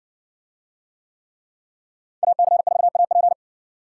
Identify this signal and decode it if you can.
Morse code, heard as a keyed tone, about 59 wpm, 700 Hz. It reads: AC4AP